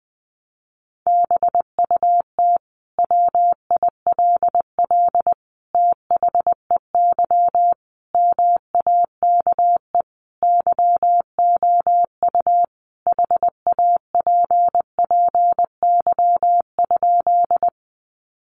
Morse code, heard as a keyed tone, about 20 wpm, 700 Hz. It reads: BUT WILL THEY MAKE YOU HAPPY?
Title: Sentences from Top 1000 Words 20wpm